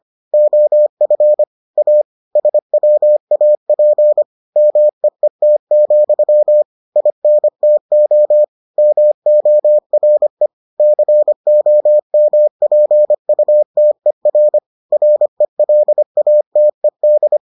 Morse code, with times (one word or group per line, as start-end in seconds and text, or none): 0.3-1.4 OF
1.8-2.0 A
2.4-4.2 SWAP
4.6-6.6 MEET,
7.0-8.4 INTO
8.8-10.5 MORE
10.8-14.6 COMPUTER
14.9-17.4 RELATED